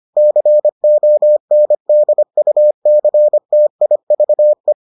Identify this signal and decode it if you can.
Morse code, heard as a keyed tone, about 25 wpm, 600 Hz. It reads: CONDUCTIVE